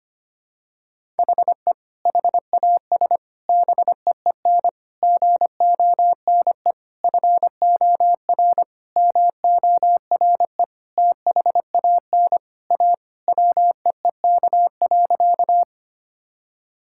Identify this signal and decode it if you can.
Morse code, heard as a keyed tone, about 25 wpm, 700 Hz. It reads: HE HAS BEEN GONE FOR MORE THAN A WEEK.
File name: practice / 25wpm / Sentences with 4-Word Phrases - 25wpm